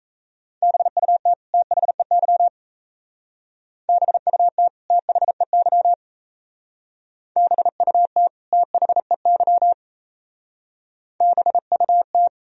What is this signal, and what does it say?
Morse code, heard as a keyed tone, about 33 wpm, 700 Hz. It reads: BUT THEY BUT THEY BUT THEY BUT
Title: Speed-Racing - Sets of 2 Words 28wpm